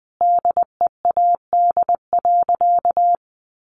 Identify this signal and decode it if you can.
Morse code, heard as a keyed tone, about 20 wpm, 700 Hz. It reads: DEAD.
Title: Sentences from Top 1500 Words 20wpm